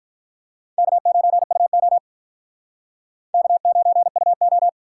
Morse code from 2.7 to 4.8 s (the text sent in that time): K9UO